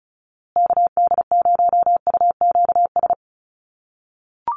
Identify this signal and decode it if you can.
Morse code, heard as a keyed tone, about 34 wpm, 700 Hz. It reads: KD0UQS
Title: USA Call Signs Repeated 3x 35wpm